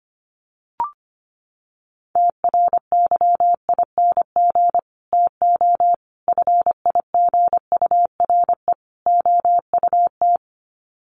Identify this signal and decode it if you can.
Morse code, heard as a keyed tone, about 25 wpm, 700 Hz. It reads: TRYING TO FIGURE OUT
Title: Sets of 4 Words 25wpm